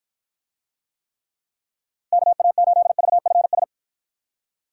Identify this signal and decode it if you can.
Morse code, heard as a keyed tone, about 52 wpm, 700 Hz. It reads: KA8VFS